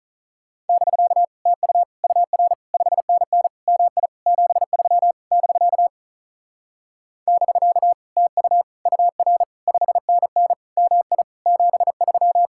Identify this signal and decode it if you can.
Morse code, heard as a keyed tone, about 46 wpm, 700 Hz. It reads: <BK> TU UR 5NN MI 73 <BK> <BK> TU UR 5NN MI 73